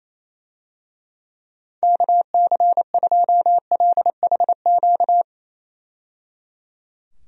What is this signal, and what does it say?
Morse code, heard as a keyed tone, about 28 wpm, 700 Hz. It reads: KC2LHQ
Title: USA Call Signs - 6 Characters - 28wpm